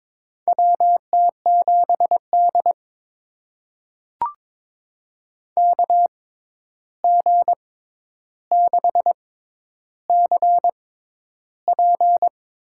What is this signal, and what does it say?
Morse code, heard as a keyed tone, about 22 wpm, spaced as for 11 wpm, 700 Hz. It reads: WT7D KG6CP
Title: USA Call Signs - 6x Character Spacing - 22wpm